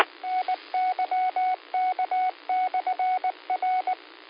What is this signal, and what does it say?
Morse code, heard as a keyed tone, about 19 wpm, 740 Hz. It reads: NYK/R
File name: qsorec SimplexLogic